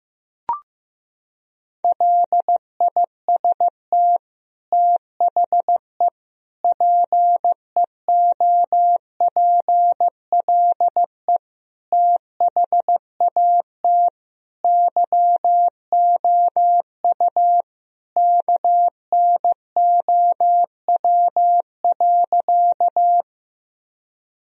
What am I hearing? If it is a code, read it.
Morse code, heard as a keyed tone, about 15 wpm, 700 Hz. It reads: LIST THE PEOPLE THAT YOU KNOW.
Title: Sentences from Top 1000 Words - Review Part 2 - 15wpm